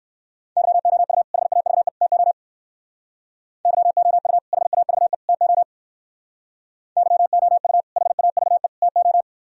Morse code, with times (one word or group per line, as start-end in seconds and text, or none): 0.6-1.2 YOU
1.3-1.9 HAVE
2.0-2.3 TO
3.7-4.4 YOU
4.5-5.1 HAVE
5.3-5.6 TO
7.0-7.8 YOU
8.0-8.7 HAVE
8.8-9.2 TO